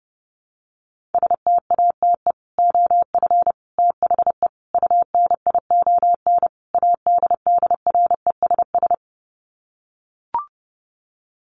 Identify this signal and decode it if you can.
Morse code, heard as a keyed tone, about 28 wpm, 700 Hz. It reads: STATE OF THE UNION ADDRESS
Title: Sets of 5 Words Encore 30wpm